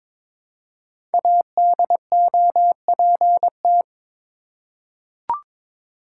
Morse code, heard as a keyed tone, about 22 wpm, 700 Hz. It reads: ADOPT
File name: 5 Letter Words - First 3 Letters Given - 22wpm